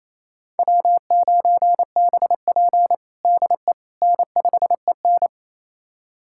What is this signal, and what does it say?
Morse code, heard as a keyed tone, about 28 wpm, 700 Hz. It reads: W9BP DE N5EN
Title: USA Callsign DE Callsign - 4 Characters - 28wpm